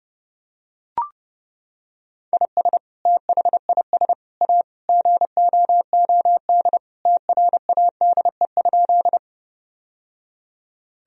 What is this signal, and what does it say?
Morse code, heard as a keyed tone, about 30 wpm, 700 Hz. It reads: IS THIS A GOOD TRADE?
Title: Sentences from Top 1000 Words - Review Part 4 - 30wpm